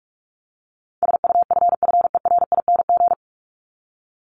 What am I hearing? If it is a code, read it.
Morse code, heard as a keyed tone, about 44 wpm, 700 Hz. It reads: SUFFERING
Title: ING Suffix - Top 100 Words with Base Words 45wpm